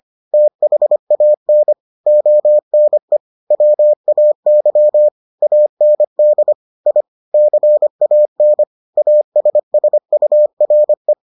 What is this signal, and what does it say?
Morse code, heard as a keyed tone, about 25 wpm, 600 Hz. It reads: THAN ONE WAY AND I CAN ASSURE